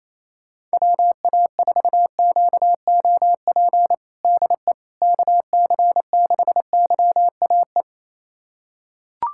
Morse code, heard as a keyed tone, about 28 wpm, 700 Hz. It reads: WA4QOP DE KC6YAE
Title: USA Callsign DE Callsign 28wpm